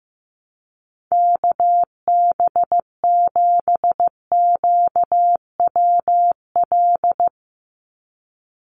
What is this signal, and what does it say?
Morse code, heard as a keyed tone, about 15 wpm, 700 Hz. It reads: KB7QWL
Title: USA Call Signs 15wpm